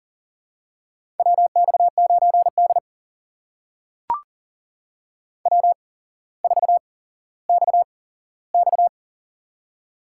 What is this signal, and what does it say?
Morse code, heard as a keyed tone, about 42 wpm, spaced as for 17 wpm, 700 Hz. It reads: WX9D W4XX
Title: USA Call Signs - 4 Characters - 8x Character Spacing - 40wpm